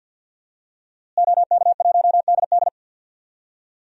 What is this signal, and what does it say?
Morse code, heard as a keyed tone, about 51 wpm, 700 Hz. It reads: OK1DD
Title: DXCC-100 Call Signs - 5 Characters - Repeated 2x - 50wpm